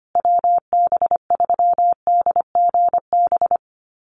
Morse code, heard as a keyed tone, about 25 wpm, 700 Hz. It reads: WB3DGB
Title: USA Call Signs Repeated 3x 25wpm